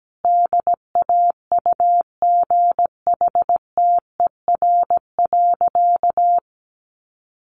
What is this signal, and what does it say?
Morse code, heard as a keyed tone, about 17 wpm, 700 Hz. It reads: DAUGHTER.
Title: Sentences from Top 1500 Words 17wpm